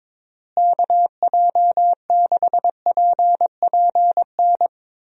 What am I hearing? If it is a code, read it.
Morse code, heard as a keyed tone, about 22 wpm, 700 Hz. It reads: KJ6PPN
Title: USA Call Signs - 6 Characters - Repeated 2x - 22wpm